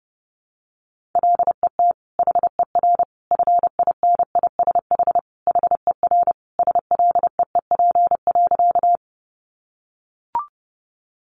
Morse code, heard as a keyed tone, about 30 wpm, 700 Hz. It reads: LET HER FINISH HER SLEEP.
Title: Sentences from Top 1200 Words 30wpm